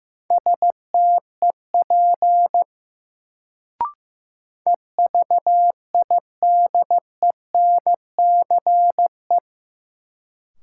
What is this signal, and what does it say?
Morse code, heard as a keyed tone, about 15 wpm, 700 Hz. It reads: STEP EVIDENCE
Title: Sets of 5 Words Encore - New Word Drill - 15wpm